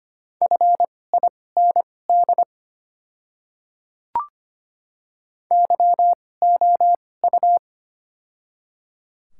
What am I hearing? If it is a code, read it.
Morse code, heard as a keyed tone, about 25 wpm, spaced as for 18 wpm, 690 Hz. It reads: FIND YOU
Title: 2x Character Spacing - Top 100 Words - 25wpm